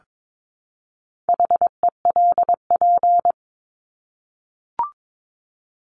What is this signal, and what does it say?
Morse code, heard as a keyed tone, about 22 wpm, 700 Hz. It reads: HELP
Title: Top 200 Words 22wpm